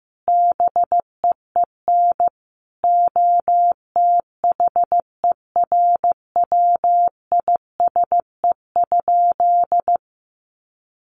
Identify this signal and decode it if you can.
Morse code, heard as a keyed tone, about 15 wpm, 700 Hz. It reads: BEEN OTHERWISE?
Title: Sentences from Top 1200 Words 15wpm